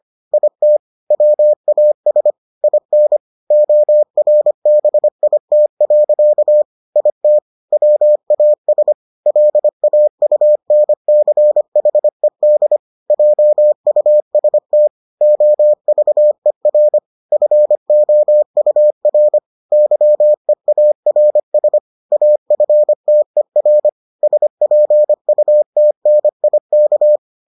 Morse code, heard as a keyed tone, about 25 wpm, 600 Hz. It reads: IT WAS IN ORBIT. IT WAS LAUNCHED JUST OVER FOUR YEARS AFTER SPUTNIK